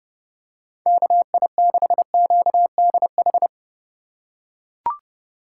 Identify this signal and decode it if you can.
Morse code, heard as a keyed tone, about 32 wpm, 700 Hz. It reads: KI6QDH
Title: K1USN SST QSO Practice - 30wpm